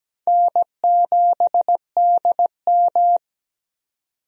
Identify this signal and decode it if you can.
Morse code, heard as a keyed tone, about 17 wpm, 700 Hz. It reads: N7DM
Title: USA Call Signs - 4 Characters - 17wpm